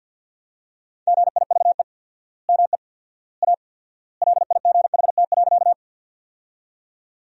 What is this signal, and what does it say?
Morse code, heard as a keyed tone, about 50 wpm, 700 Hz. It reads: GIVE ME A LIGHT.
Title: 4x Word-Spacing - Sentences from Top 200 Words - 50wpm